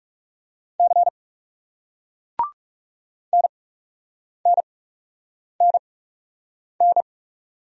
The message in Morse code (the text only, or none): C NNNN